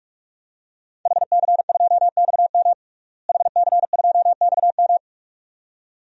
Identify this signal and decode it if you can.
Morse code, heard as a keyed tone, about 41 wpm, 700 Hz. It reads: HC2XM HC2XM
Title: DXCC-100 Call Signs - 5 Characters - Repeated 2x - 45wpm